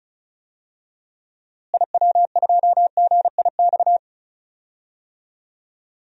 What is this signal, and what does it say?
Morse code, heard as a keyed tone, about 36 wpm, 700 Hz. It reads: IW2GIX
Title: DXCC-100 Call Signs - 6 Characters - Repeated 2x - 35wpm